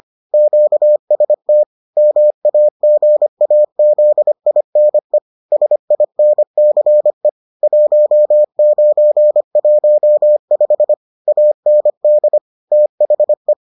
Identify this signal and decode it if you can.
Morse code, heard as a keyed tone, about 25 wpm, 600 Hz. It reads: QST MAGAZINE SINCE 1915 AND THE